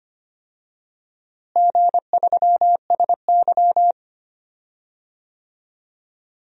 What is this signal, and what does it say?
Morse code, heard as a keyed tone, about 25 wpm, 700 Hz. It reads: G3SY
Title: DXCC-100 Call Signs - 4 Characters - Repeated 2x - 25wpm